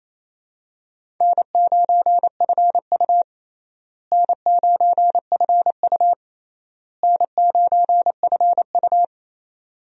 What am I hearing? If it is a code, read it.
Morse code, heard as a keyed tone, about 28 wpm, 700 Hz. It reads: N9FU N9FU N9FU
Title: USA Call Signs - 4 Characters - Repeated 3x - 28wpm